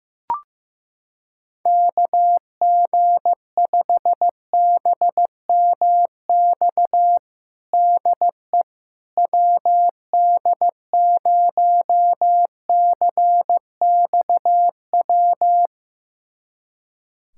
Morse code, heard as a keyed tone, about 15 wpm, 700 Hz. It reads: KG5BMX DE WD0CXW
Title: USA Callsign DE Callsign - 6 Characters - 15wpm